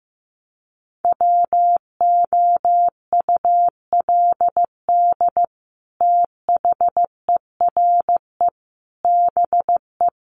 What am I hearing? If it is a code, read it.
Morse code, heard as a keyed tone, about 15 wpm, 700 Hz. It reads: WOULD THERE BE
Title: Sentences from Top 1500 Words 15wpm